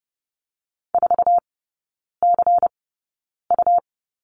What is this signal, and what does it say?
Morse code, heard as a keyed tone, about 30 wpm, spaced as for 9 wpm, 700 Hz. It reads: 4CU
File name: Lesson 17e - Letter u - Three Characters 30wpm